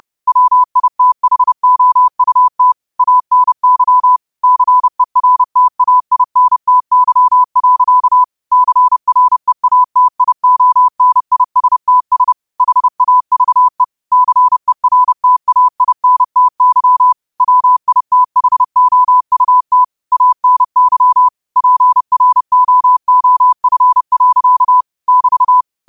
Morse code, heard as a keyed tone, about 30 wpm, 1000 Hz. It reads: WITHOUT ANY CERTAINTY. CREATIONISTS HAVE CERTAINTY WITHOUT ANY PROOF. <BT>
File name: qotdfiles / 30wpm